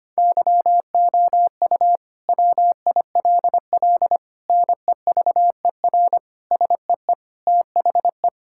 Morse code, heard as a keyed tone, about 25 wpm, 700 Hz. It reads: YOU WILL NEVER SEE THE
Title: Sentences from Top 1600 Words 25wpm